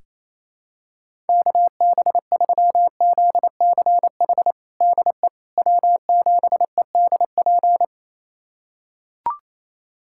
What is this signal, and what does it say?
Morse code, heard as a keyed tone, about 27 wpm, 700 Hz. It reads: KB3ZCH DE W7EDP
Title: USA Callsign DE Callsign 28wpm